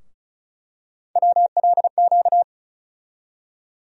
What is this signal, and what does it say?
Morse code, heard as a keyed tone, about 34 wpm, 700 Hz. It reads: WLQ
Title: USA Call Sign Postfixes - 3 Letters 35wpm